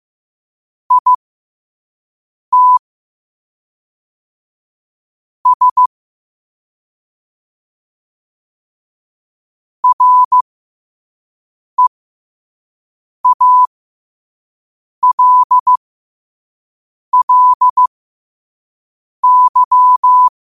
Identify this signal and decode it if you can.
Morse code, heard as a keyed tone, about 15 wpm, spaced as for 5 wpm, 1000 Hz. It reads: IT S REALLY
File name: qotdfiles / 5wpm